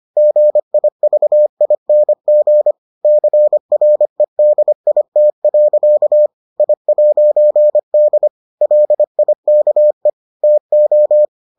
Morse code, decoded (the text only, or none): GIVING CREDIT. I'D LIKE TO